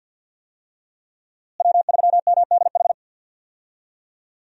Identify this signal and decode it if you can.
Morse code, heard as a keyed tone, about 52 wpm, 700 Hz. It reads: W3MDH
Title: USA Call Signs - 5 Characters - Repeated 3x - 50wpm